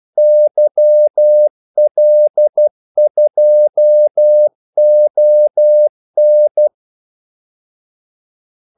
Morse code, heard as a keyed tone, about 12 wpm, 600 Hz. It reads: YL2ON
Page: 2 ekstra ordmellemrum mellem kaldesignalerne